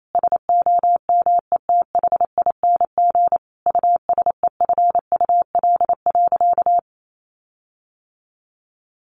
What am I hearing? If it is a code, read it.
Morse code, heard as a keyed tone, about 28 wpm, 700 Hz. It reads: SOMETHING USEFUL.
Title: Sentences from Top 1400 Words 28wpm